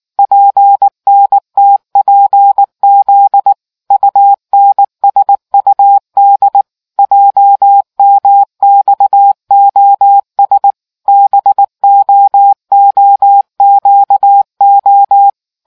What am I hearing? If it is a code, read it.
Morse code, heard as a keyed tone, about 19 wpm, 800 Hz. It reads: PNTPZ UNSUD JMXOS BOOQO